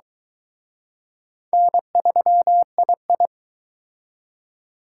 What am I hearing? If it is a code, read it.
Morse code, heard as a keyed tone, about 23 wpm, 700 Hz. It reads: N3II